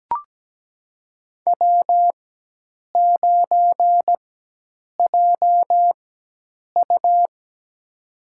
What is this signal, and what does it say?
Morse code, heard as a keyed tone, about 17 wpm, spaced as for 8 wpm, 700 Hz. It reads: W9JU
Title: USA Call Signs - 4 Characters - 4x Character Spacing - 17wpm